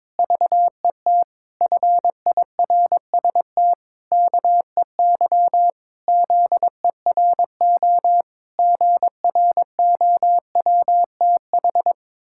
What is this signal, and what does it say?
Morse code, heard as a keyed tone, about 21 wpm, 700 Hz. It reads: VET FIRST KEY ZERO GROWTH
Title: Pangram Challenge v2 - 22wpm